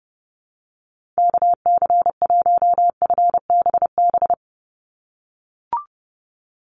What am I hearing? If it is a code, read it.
Morse code, heard as a keyed tone, about 31 wpm, 700 Hz. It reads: KC1FBB
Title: USA Call Signs 30wpm